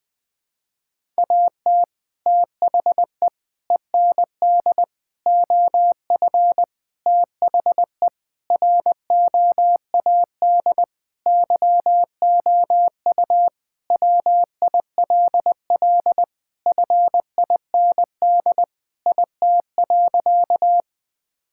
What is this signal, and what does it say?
Morse code, heard as a keyed tone, about 20 wpm, 700 Hz. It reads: AT THE END OF THE ROAD YOU WILL FIND IT.
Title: Sentences with 3-Word Phrases - 20wpm